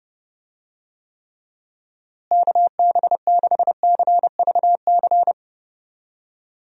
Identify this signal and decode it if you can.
Morse code, heard as a keyed tone, about 30 wpm, 700 Hz. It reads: KB6CVC
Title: USA Call Signs - 6 Characters - Repeated 2x - 30wpm